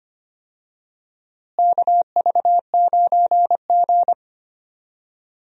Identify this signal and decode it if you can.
Morse code, heard as a keyed tone, about 25 wpm, 700 Hz. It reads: KV9G
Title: USA Call Signs - 4 Characters - Repeated 3x - 25wpm